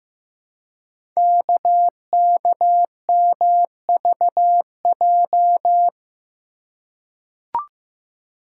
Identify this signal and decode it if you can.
Morse code, heard as a keyed tone, about 15 wpm, 700 Hz. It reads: KKMVJ
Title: Code Groups - 5 Letters - 15wpm